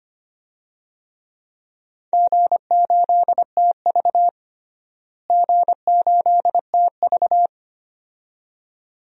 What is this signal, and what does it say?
Morse code, heard as a keyed tone, about 25 wpm, 700 Hz. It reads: G8TV G8TV
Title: DXCC-100 Call Signs - 4 Characters - Repeated 2x - 25wpm